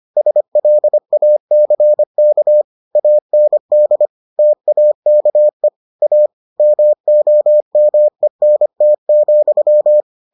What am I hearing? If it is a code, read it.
Morse code, heard as a keyed tone, about 25 wpm, 600 Hz. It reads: SLACK AND TAKE A MOMENT,